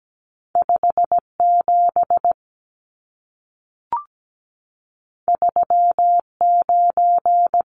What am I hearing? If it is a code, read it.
Morse code, heard as a keyed tone, about 17 wpm, 700 Hz. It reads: 57 39
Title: Numbers - 2-Digit 17wpm